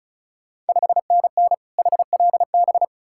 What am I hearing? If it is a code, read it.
Morse code, heard as a keyed tone, about 36 wpm, 690 Hz. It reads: 5NN HLB
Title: Ontario QSO Party Exchange 35wpm